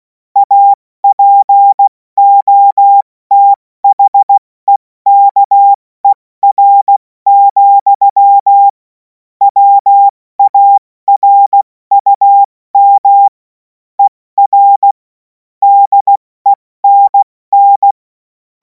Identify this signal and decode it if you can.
Morse code, heard as a keyed tone, about 16 wpm, 800 Hz. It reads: APOTHEKER, WARUM ER DENN
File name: Morsekurs / Witze / Witz 17. März 2026 70 F80